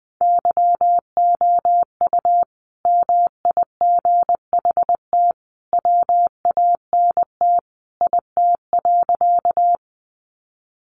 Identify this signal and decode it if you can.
Morse code, heard as a keyed tone, about 20 wpm, 700 Hz. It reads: YOU MIGHT WANT IT.
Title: Sentences from Top 300 Words 20wpm